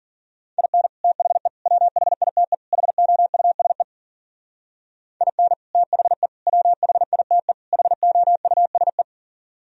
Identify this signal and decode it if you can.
Morse code, heard as a keyed tone, about 52 wpm, 700 Hz. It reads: IN THE WHITE HOUSE IN THE WHITE HOUSE